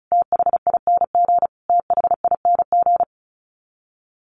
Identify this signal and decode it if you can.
Morse code, heard as a keyed tone, about 34 wpm, 700 Hz. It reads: THING THING
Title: Top 200 Words Repeated 35wpm